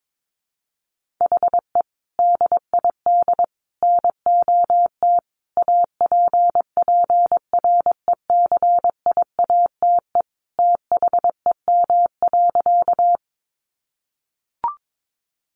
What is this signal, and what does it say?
Morse code, heard as a keyed tone, about 22 wpm, 700 Hz. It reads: HE DID NOT APPRECIATE THEM.
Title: Sentences from Top 1300 Words 22wpm